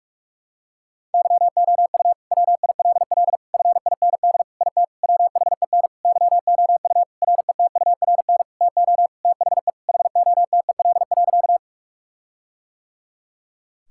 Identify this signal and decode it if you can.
Morse code, heard as a keyed tone, about 46 wpm, 700 Hz. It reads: YOU WILL FIND IT WHEN YOU RETURN TO THE HOTEL.
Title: Sentences from Top 1600 Words 45wpm